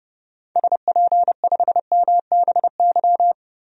Morse code, heard as a keyed tone, about 30 wpm, 700 Hz. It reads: SP5MBY
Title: DXCC-100 Call Signs - 6 Characters - 30wpm